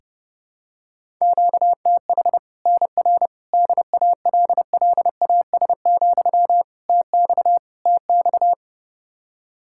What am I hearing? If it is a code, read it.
Morse code, heard as a keyed tone, about 30 wpm, 700 Hz. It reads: QTH NR DALLAS, TX TX
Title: Practice QSO - Exchange 1 - Beginning - 30wpm